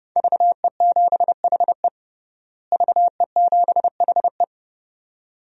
Morse code, heard as a keyed tone, about 30 wpm, 700 Hz. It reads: VE7HE VE7HE
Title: DXCC-100 Call Signs - 5 Characters - Repeated 2x - 30wpm